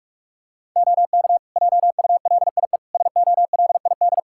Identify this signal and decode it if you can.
Morse code, heard as a keyed tone, about 47 wpm, 700 Hz. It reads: OK JULIE SOLID
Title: Practice QSO - Exchange 3 - More Info - 45wpm